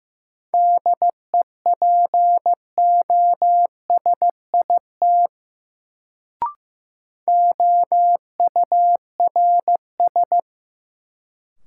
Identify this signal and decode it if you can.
Morse code, heard as a keyed tone, about 15 wpm, 700 Hz. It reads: DEPOSIT OURS
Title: Sentences from Top 1300 Words - New Word Drill Alt - 15wpm